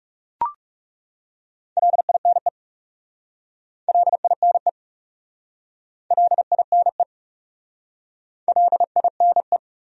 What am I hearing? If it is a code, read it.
Morse code, heard as a keyed tone, about 38 wpm, 700 Hz. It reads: LINE LINE LINE LINE